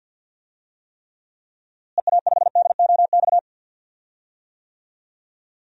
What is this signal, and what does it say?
Morse code, heard as a keyed tone, about 47 wpm, 700 Hz. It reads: EA5DOX
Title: DXCC-100 Call Signs - 6 Characters - Repeated 2x - 50wpm